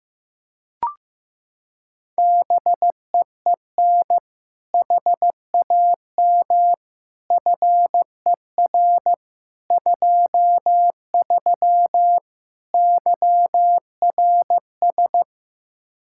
Morse code, heard as a keyed tone, about 16 wpm, 700 Hz. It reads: BEEN HAM FER 23 YRS
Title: Practice QSO - Exchange 3 - More Info - 15wpm